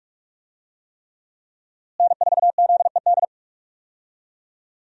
Morse code, heard as a keyed tone, about 46 wpm, 700 Hz. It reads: N4ZED
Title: USA Call Signs - 5 Characters - Repeated 3x - 45wpm